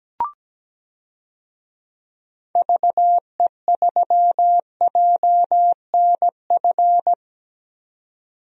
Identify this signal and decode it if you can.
Morse code, heard as a keyed tone, about 17 wpm, 700 Hz. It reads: VE3JNF
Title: DXCC-100 Call Signs - 6 Characters - 17wpm